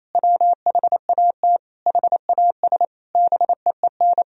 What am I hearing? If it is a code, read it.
Morse code, heard as a keyed tone, about 28 wpm, 700 Hz. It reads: WHAT HAS BEEN
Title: Sentences from Top 1000 Words - Review Part 1 - 28wpm